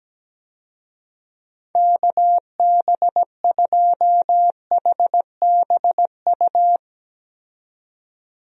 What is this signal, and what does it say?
Morse code, heard as a keyed tone, about 17 wpm, 700 Hz. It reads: KB2HBU
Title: USA Call Signs - 6 Characters - 17wpm